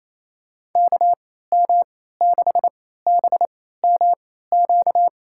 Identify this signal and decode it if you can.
Morse code, heard as a keyed tone, about 28 wpm, spaced as for 16 wpm, 700 Hz. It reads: KM6BMQ
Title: USA Call Signs - 6 Characters - 3x Character Spacing - 28wpm